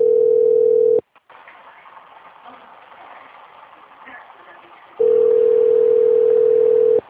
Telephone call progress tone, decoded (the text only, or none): ringback tone